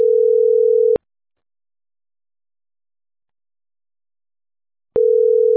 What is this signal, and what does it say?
Ringback tone (North America), 440 + 480 Hz.